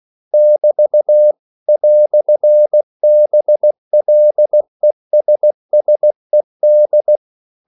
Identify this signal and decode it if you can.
Morse code, heard as a keyed tone, about 16 wpm, 600 Hz. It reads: <BT> "BLESSED
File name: CW / Practice / EnglishQuotations0216